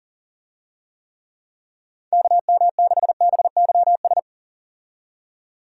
Morse code, heard as a keyed tone, about 40 wpm, 700 Hz. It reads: KM6BYS
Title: USA Call Signs - 6 Characters - 40wpm